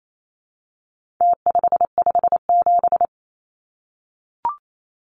Morse code, heard as a keyed tone, about 28 wpm, 700 Hz. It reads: T557
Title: Numbers - Contest Style - 4-Digit 28wpm